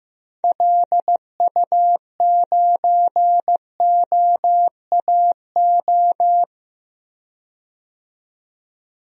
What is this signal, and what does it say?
Morse code, heard as a keyed tone, about 16 wpm, 700 Hz. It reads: LU9OAO
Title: DXCC-100 Call Signs - 6 Characters - 15wpm